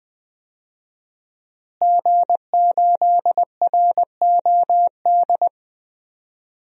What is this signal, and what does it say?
Morse code, heard as a keyed tone, about 21 wpm, 700 Hz. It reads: G8ROD
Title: DXCC-100 Call Signs - 5 Characters - Repeated 3x - 20wpm